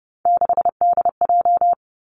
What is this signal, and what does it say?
Morse code, heard as a keyed tone, about 30 wpm, 700 Hz. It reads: 6DJ
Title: Three Challenging Letter-Number Combinations 30wpm